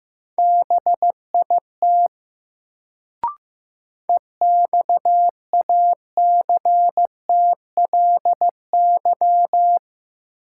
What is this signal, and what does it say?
Morse code, heard as a keyed tone, about 15 wpm, 700 Hz. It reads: BIT EXACTLY
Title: Sentences from Top 900 Words - New Word Drill - 15wpm